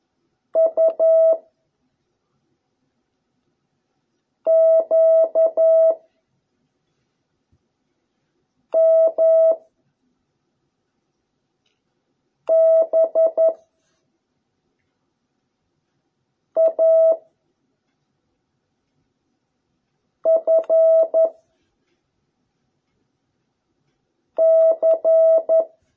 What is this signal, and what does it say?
Morse code, heard as a keyed tone, about 11 wpm, 650 Hz. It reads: U Q M B A F C